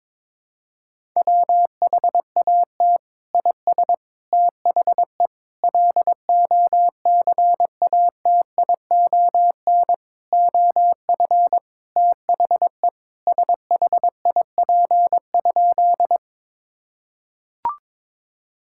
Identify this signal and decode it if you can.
Morse code, heard as a keyed tone, about 22 wpm, 700 Hz. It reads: WHAT IS THE LOCATION OF THE SHIP?
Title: Sentences from Top 1000 Words - Review Part 6 - 22wpm